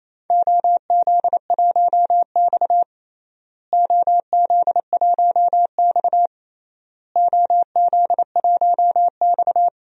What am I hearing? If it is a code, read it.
Morse code, heard as a keyed tone, about 28 wpm, 700 Hz. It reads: OZ1X OZ1X OZ1X
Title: DXCC-100 Call Signs - 4 Characters - Repeated 3x - 28wpm